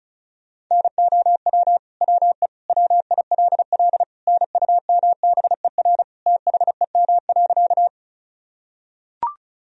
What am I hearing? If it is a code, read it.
Morse code, heard as a keyed tone, about 36 wpm, 700 Hz. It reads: NOW WE WILL NUMBER THEM.
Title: Sentences from Top 1000 Words - Review Part 1 - 35wpm